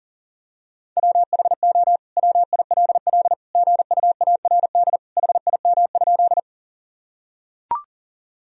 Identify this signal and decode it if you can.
Morse code, heard as a keyed tone, about 40 wpm, 700 Hz. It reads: WHO WILL GUARD HIM?
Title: Sentences from Top 1000 Words - Review Part 4 - 40wpm